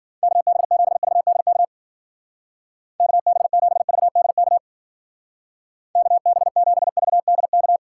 Morse code, heard as a keyed tone, about 49 wpm, 700 Hz. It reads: KB7VDK KB7VDK KB7VDK
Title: Speed Racing - USA 2x3 Call Signs 40wpm